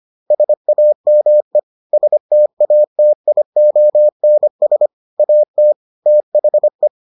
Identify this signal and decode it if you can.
Morse code, heard as a keyed tone, about 25 wpm, 600 Hz. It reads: SAME STATIONS AT THE